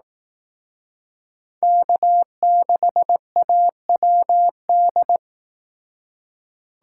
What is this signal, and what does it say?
Morse code, heard as a keyed tone, about 18 wpm, 700 Hz. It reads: K6AWD
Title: Speed Racing - USA Call Signs 15wpm